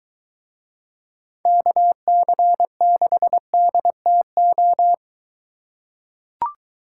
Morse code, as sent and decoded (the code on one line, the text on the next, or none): -.- -.-. -.... -.. - ---
KC6DTO